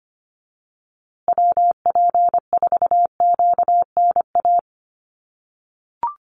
Morse code, heard as a keyed tone, about 25 wpm, 700 Hz. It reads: WP4QNA
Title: USA Call Signs Repeated 3x 25wpm